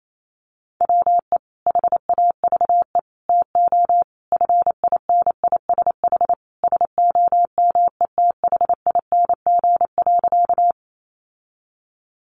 Morse code, heard as a keyed tone, about 28 wpm, 700 Hz. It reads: WE HAVE TO FINISH SOMETHING.
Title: Sentences from Top 1200 Words 28wpm